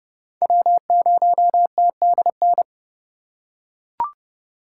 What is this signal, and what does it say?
Morse code, heard as a keyed tone, about 29 wpm, 700 Hz. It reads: W0TDN
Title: USA Call Signs - 5 Characters - 2x Character Spacing - 30wpm